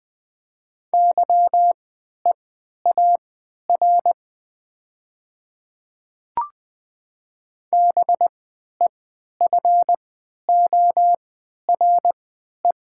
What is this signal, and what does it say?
Morse code, heard as a keyed tone, about 20 wpm, spaced as for 11 wpm, 700 Hz. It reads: YEAR BEFORE
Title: 3x Character Spacing - Top 200 Words - 20wpm